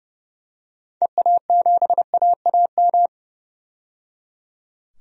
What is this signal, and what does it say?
Morse code, heard as a keyed tone, about 30 wpm, 700 Hz. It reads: EA7AAM
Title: DXCC-100 Call Signs - 6 Characters - 30wpm